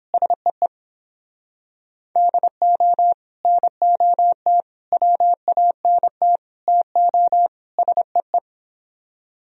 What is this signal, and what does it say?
Morse code, heard as a keyed tone, about 30 wpm, 700 Hz. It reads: SEE DO NOT WANT TO SEE